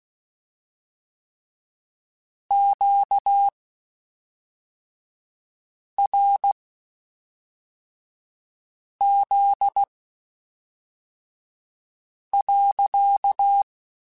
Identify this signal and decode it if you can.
Morse code, heard as a keyed tone, about 16 wpm, spaced as for 3 wpm, 800 Hz. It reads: QRZ.
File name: Q Calls / Q Calls-mp3s / Q Calls sppc @5